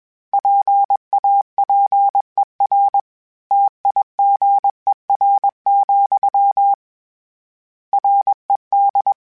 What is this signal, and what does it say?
Morse code, heard as a keyed tone, about 21 wpm, 800 Hz. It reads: PAPER TIGER, RED
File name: Misc / Misc-mp3s / Word Pairs @20